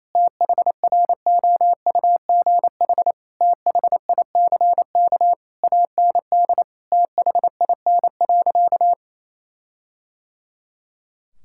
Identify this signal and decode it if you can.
Morse code, heard as a keyed tone, about 28 wpm, 700 Hz. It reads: THROUGH THICK AND THIN.
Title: Sentences from Top 1600 Words 28wpm